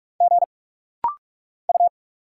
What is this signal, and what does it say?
Morse code, heard as a keyed tone, about 46 wpm, 700 Hz. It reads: G U
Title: Single Letter-Number Alt - Rapid-Fire 45wpm